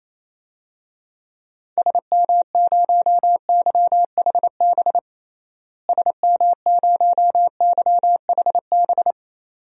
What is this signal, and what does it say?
Morse code, heard as a keyed tone, about 28 wpm, 700 Hz. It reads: SM0YHB SM0YHB
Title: DXCC-100 Call Signs - 6 Characters - Repeated 2x - 28wpm